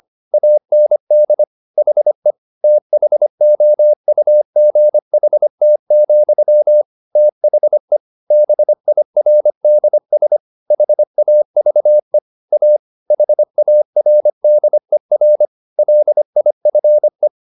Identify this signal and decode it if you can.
Morse code, heard as a keyed tone, about 25 wpm, 600 Hz. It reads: AND HE THOUGHT, THE BIRDS HAVE A HARDER LIFE
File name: the old man and the sea part 1 25wpm